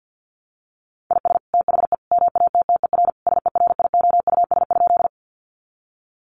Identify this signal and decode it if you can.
Morse code, heard as a keyed tone, about 50 wpm, 700 Hz. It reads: IS THE MATTER SERIOUS?
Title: Sentences from Top 1100 Words 50wpm